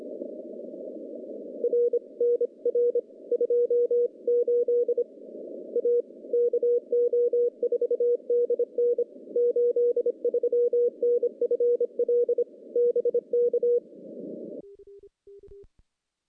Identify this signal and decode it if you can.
Morse code, heard as a keyed tone, about 26 wpm, 480 Hz. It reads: R NR 28 A KO4DN 83 NFL BK
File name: contests / CW-SS-k5tr- -0400